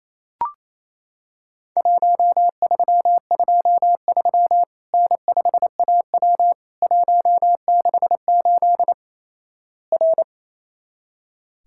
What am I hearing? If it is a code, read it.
Morse code, heard as a keyed tone, about 28 wpm, 700 Hz. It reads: 1323 N5AW 168 R
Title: Worked All Europe DX Contest - QTC Exchange - 28wpm